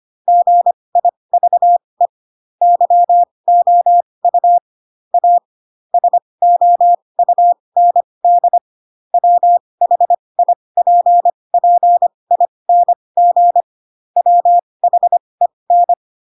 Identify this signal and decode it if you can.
Morse code, heard as a keyed tone, about 25 wpm, spaced as for 20 wpm, 700 Hz. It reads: GIVE YOU A SOUND WHIPPING WHEN
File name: Pinocchio / Pinocchio ch7 15 25 ik1hge